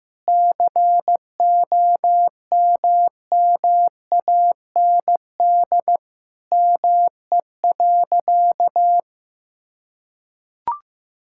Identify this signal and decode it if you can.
Morse code, heard as a keyed tone, about 15 wpm, 700 Hz. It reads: COMMAND ME.
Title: Sentences from Top 1600 Words 15wpm